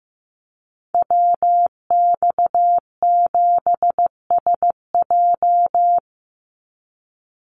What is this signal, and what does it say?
Morse code, heard as a keyed tone, about 15 wpm, 700 Hz. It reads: WX7SJ
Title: Call Signs Repeated 2x - 15wpm